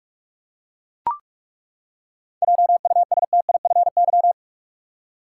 Morse code, heard as a keyed tone, about 42 wpm, 700 Hz. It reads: JUSTIFY
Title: Seven Letter Words - 45wpm